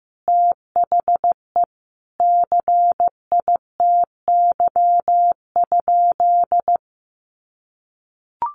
Morse code, heard as a keyed tone, about 15 wpm, 700 Hz. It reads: THE CITY?
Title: Sentences from Top 1300 Words 15wpm